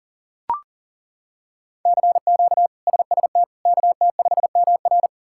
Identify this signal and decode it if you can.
Morse code, heard as a keyed tone, about 40 wpm, 690 Hz. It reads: CQ SST KT5MR
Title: K1USN SST QSO Practice - 40wpm